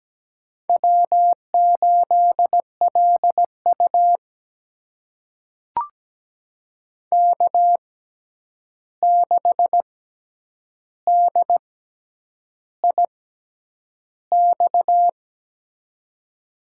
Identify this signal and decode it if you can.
Morse code, heard as a keyed tone, about 17 wpm, spaced as for 8 wpm, 700 Hz. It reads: W8LU K6DIX
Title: USA Call Signs - 6x Character Spacing - 17wpm